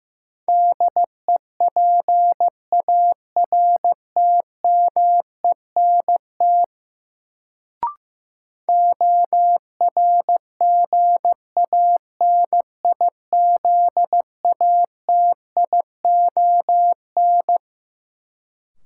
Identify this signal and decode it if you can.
Morse code, heard as a keyed tone, about 15 wpm, 700 Hz. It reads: DEPARTMENT ORGANIZATION
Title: Sentences from Top 1000 Words - Review Part 5 - New Word Drill - 15wpm